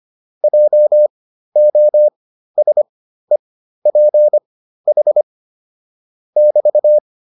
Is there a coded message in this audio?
Yes: Morse code from 0.4 to 7.0 s, heard as a keyed tone, about 25 wpm, spaced as for 13 wpm, 600 Hz. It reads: JOSEPH <BT>